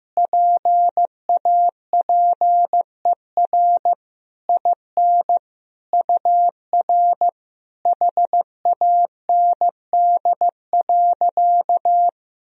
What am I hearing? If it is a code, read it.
Morse code, heard as a keyed tone, about 16 wpm, 700 Hz. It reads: PAPER IN UR HAND.
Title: Sentences from Top 400 Words and Abbreviations 15wpm